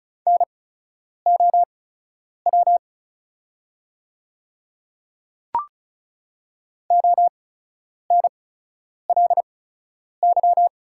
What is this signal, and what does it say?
Morse code, heard as a keyed tone, about 34 wpm, spaced as for 10 wpm, 700 Hz. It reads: NOW ONLY
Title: 8x Character Spacing - Top 300 Words - 35wpm